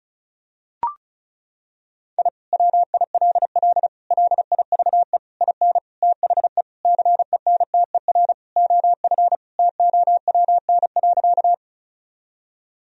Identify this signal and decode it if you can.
Morse code, heard as a keyed tone, about 35 wpm, 700 Hz. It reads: I WILL LIVE IN THE CENTER OF TOWN.
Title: Sentences from Top 1000 Words - Review Part 2 - 35wpm